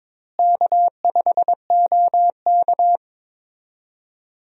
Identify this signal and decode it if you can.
Morse code, heard as a keyed tone, about 21 wpm, 700 Hz. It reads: K5OK
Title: USA Call Signs - 4 Characters - Repeated 2x - 22wpm